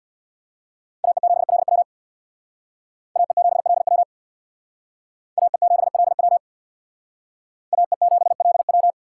Morse code, heard as a keyed tone, about 64 wpm, 700 Hz. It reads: AE7LW AE7LW AE7LW AE7LW